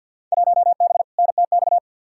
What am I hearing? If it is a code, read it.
Morse code, heard as a keyed tone, about 46 wpm, 700 Hz. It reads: 1B NTX
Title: Field Day Exchanges - 50wpm